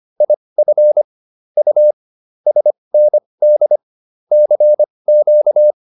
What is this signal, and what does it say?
Morse code, heard as a keyed tone, about 25 wpm, spaced as for 20 wpm, 600 Hz. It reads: IF U SND CQ